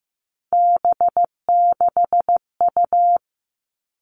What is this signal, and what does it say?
Morse code, heard as a keyed tone, about 15 wpm, 700 Hz. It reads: B6U
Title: Three Challenging Letter-Number Combinations 15wpm